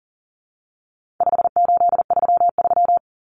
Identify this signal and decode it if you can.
Morse code, heard as a keyed tone, about 40 wpm, 700 Hz. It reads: 5833
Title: Numbers - 4-Digit 40wpm